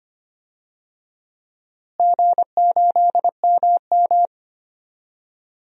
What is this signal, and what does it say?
Morse code, heard as a keyed tone, about 25 wpm, 700 Hz. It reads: G8MM
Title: DXCC-100 Call Signs - 4 Characters - Repeated 3x - 25wpm